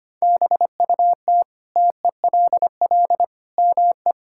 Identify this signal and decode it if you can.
Morse code, heard as a keyed tone, about 25 wpm, 700 Hz. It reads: BUT TELL ME
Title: Sentences from Top 1000 Words - Review Part 6 - 25wpm